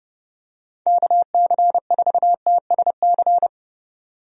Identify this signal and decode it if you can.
Morse code, heard as a keyed tone, about 30 wpm, 700 Hz. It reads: KC4TSC
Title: USA Call Signs - 6 Characters - 30wpm